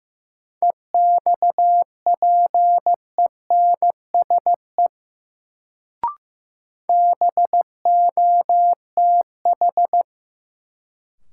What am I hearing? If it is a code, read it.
Morse code, heard as a keyed tone, about 15 wpm, 700 Hz. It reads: EXPENSE BOTH
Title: Sets of 4 Words - New Word Drill - 15wpm